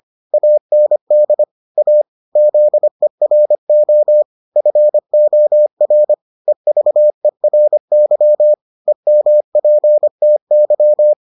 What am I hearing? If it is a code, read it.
Morse code, heard as a keyed tone, about 25 wpm, 600 Hz. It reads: AND A ZERO FOR EVERY EMPTY